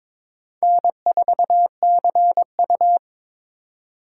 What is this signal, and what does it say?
Morse code, heard as a keyed tone, about 21 wpm, 700 Hz. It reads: N4CU
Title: USA Call Signs - 4 Characters - 3x Character Spacing - 22wpm